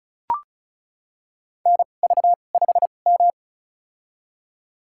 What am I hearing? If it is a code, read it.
Morse code, heard as a keyed tone, about 36 wpm, spaced as for 25 wpm, 700 Hz. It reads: NV5M
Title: USA Call Signs - 4 Characters - 2x Character Spacing - 35wpm